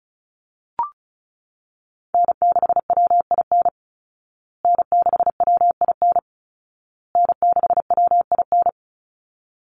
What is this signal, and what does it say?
Morse code, heard as a keyed tone, about 34 wpm, 700 Hz. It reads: N6WIN N6WIN N6WIN
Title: Call Signs Repeated 2x - 35wpm